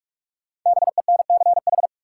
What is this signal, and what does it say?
Morse code, heard as a keyed tone, about 44 wpm, 700 Hz. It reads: BENCH
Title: Sentences from Top 1400 Words - New Word Drill - 45wpm